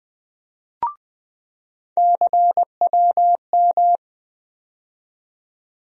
Morse code, heard as a keyed tone, about 20 wpm, 700 Hz. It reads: CWM
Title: USA Call Sign Postfixes - 3 Letters 20wpm